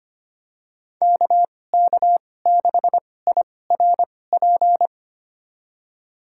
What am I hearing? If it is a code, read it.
Morse code, heard as a keyed tone, about 25 wpm, spaced as for 18 wpm, 700 Hz. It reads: KK6IRP
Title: USA Call Signs - 6 Characters - 2x Character Spacing - 25wpm